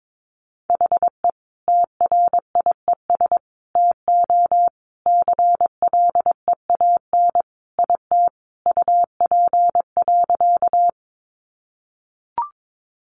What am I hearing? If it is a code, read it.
Morse code, heard as a keyed tone, about 22 wpm, 700 Hz. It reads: HE TRIES TO CLEAN IT UP.
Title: Additional Easy Phrases 22wpm